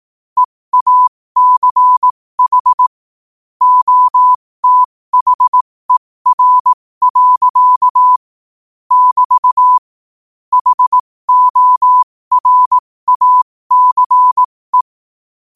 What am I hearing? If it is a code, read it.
Morse code, heard as a keyed tone, about 18 wpm, spaced as for 15 wpm, 1000 Hz. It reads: EACH OTHER. <BT> HORACE
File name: qotdfiles / 15wpm